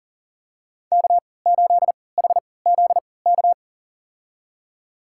Morse code, heard as a keyed tone, about 40 wpm, spaced as for 23 wpm, 700 Hz. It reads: K8HZK
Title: USA Call Signs - 5 Characters - 3x Character Spacing - 40wpm